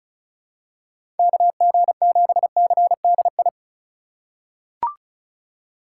Morse code, heard as a keyed tone, about 36 wpm, 700 Hz. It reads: KG7CDI
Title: USA Call Signs - 6 Characters - 4x Character Spacing - 35wpm